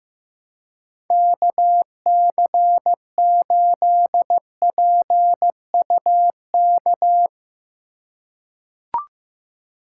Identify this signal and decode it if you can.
Morse code, heard as a keyed tone, about 16 wpm, 700 Hz. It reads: KC8PUK
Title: USA Call Signs - 6 Characters - 3x Character Spacing - 15wpm